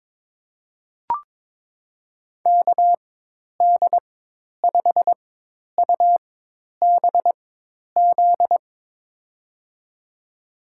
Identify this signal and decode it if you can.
Morse code, heard as a keyed tone, about 22 wpm, spaced as for 10 wpm, 700 Hz. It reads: KD5UBZ
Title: USA Call Signs - 6 Characters - 4x Character Spacing - 22wpm